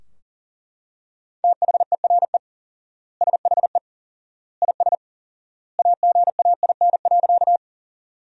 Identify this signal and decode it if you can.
Morse code, heard as a keyed tone, about 40 wpm, 700 Hz. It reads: THERE SHE IS AGAIN.
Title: 4x Word-Spacing - Sentences from Top 200 Words - 40wpm